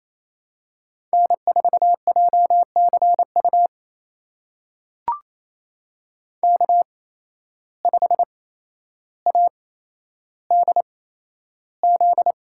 Morse code, heard as a keyed tone, about 28 wpm, spaced as for 12 wpm, 700 Hz. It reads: N4JCU K5ADZ
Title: USA Call Signs - 5 Characters - 8x Character Spacing - 28wpm